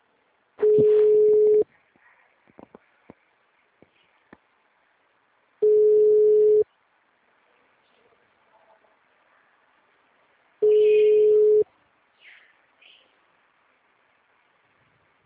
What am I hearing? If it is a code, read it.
Ringback tone (Europe, 425 Hz).
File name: out- -2048- - - .34702